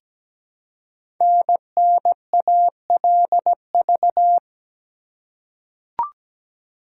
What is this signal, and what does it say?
Morse code, heard as a keyed tone, about 17 wpm, 700 Hz. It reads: NNALV
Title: Code Groups - 5 Letter:Numbers - 17wpm